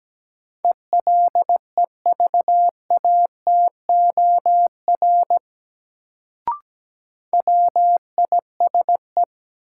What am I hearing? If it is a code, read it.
Morse code, heard as a keyed tone, about 17 wpm, 700 Hz. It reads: ELEVATOR WISE
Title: Sentences from Top 1600 Words - New Word Drill Alt - 17wpm